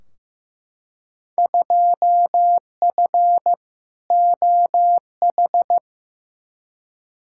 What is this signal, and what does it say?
Morse code, heard as a keyed tone, about 15 wpm, 700 Hz. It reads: 2F OH
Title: Field Day Exchanges - 15wpm